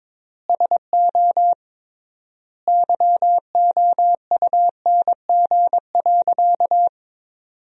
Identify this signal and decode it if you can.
Morse code, heard as a keyed tone, about 22 wpm, 700 Hz. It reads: SO YOUNG.
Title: 3x Word-Spacing - Sentences from Top 400 Words - 22wpm